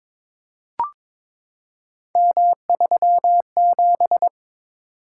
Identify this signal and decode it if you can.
Morse code, heard as a keyed tone, about 22 wpm, 700 Hz. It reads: M37
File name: practice / 22wpm / Three Letter-Number Combinations - 22wpm